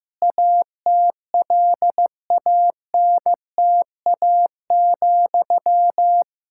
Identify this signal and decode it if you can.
Morse code, heard as a keyed tone, about 16 wpm, 700 Hz. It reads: ATLANTA,
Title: K1USN SST QSO Practice 15wpm